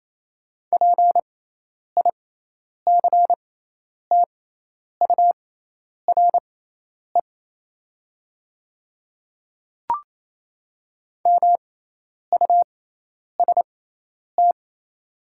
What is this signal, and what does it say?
Morse code, heard as a keyed tone, about 27 wpm, spaced as for 10 wpm, 700 Hz. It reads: PICTURE MUST
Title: 6x Character Spacing - Top 200 Words - 28wpm